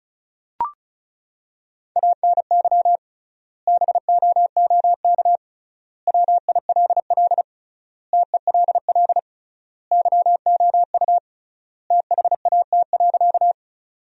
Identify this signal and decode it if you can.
Morse code, heard as a keyed tone, about 34 wpm, 700 Hz. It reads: ANY BOOK WILL TELL YOU THAT.
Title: 3x Word-Spacing - Sentences from Top 300 Words - 35wpm